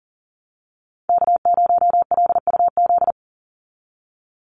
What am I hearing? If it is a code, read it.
Morse code, heard as a keyed tone, about 40 wpm, 700 Hz. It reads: K0LUZ
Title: Call Signs Repeated 2x - 40wpm